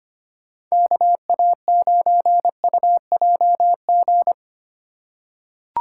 Morse code, heard as a keyed tone, about 25 wpm, 700 Hz. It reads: KA9UJG
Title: USA Call Signs - 6 Characters - 3x Character Spacing - 25wpm